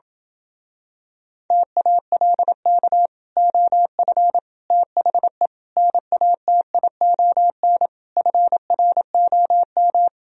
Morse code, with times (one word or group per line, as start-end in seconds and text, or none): 1.5-3.1 TALK
3.4-4.4 OF
4.7-5.5 THE
5.8-7.9 NATION
8.2-10.1 FROM